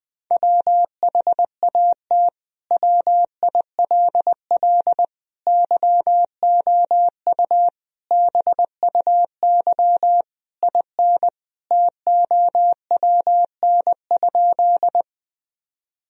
Morse code, heard as a keyed tone, about 20 wpm, 700 Hz. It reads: WHAT WILL YOU BUY IN TOWN?
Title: Sentences from Top 1000 Words - Review Part 4 - 20wpm